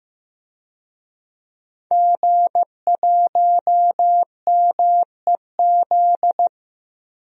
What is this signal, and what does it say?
Morse code, heard as a keyed tone, about 15 wpm, 700 Hz. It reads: G1MEZ
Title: DXCC-100 Call Signs - 5 Characters - Repeated 2x - 15wpm